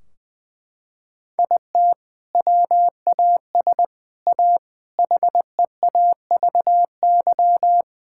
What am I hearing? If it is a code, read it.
Morse code, heard as a keyed tone, about 20 wpm, 700 Hz. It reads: IT WAS A HEAVY
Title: Sentences from Top 1600 Words 20wpm